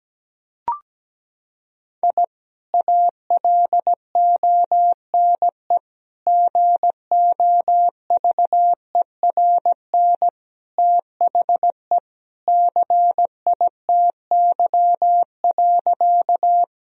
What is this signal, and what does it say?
Morse code, heard as a keyed tone, about 17 wpm, 700 Hz. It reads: I ALONE GOVERN THE CITY.
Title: Sentences from Top 1000 Words - Review Part 2 - 17wpm